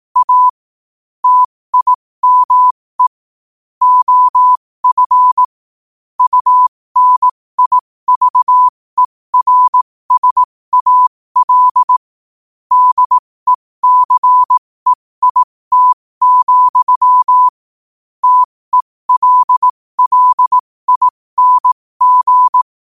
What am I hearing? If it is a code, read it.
Morse code, heard as a keyed tone, about 18 wpm, spaced as for 15 wpm, 1000 Hz. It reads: A TIME OF UNIVERSAL DECEIT, TELLING
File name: qotdfiles / 15wpm